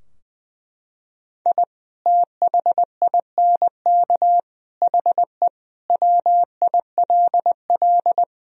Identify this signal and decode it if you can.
Morse code, heard as a keyed tone, about 20 wpm, 700 Hz. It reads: I THINK HE WILL
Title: Sentences from Top 1000 Words - Review Part 3 - 20wpm